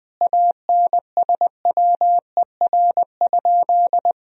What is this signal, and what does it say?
Morse code, heard as a keyed tone, about 20 wpm, 700 Hz. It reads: ANSWER?
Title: 3x Word-Spacing - Sentences from Top 300 Words - 20wpm